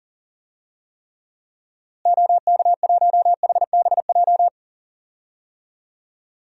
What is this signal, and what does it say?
Morse code, heard as a keyed tone, about 40 wpm, 700 Hz. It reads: OK1HBJ
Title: DXCC-100 Call Signs - 6 Characters - Repeated 3x - 40wpm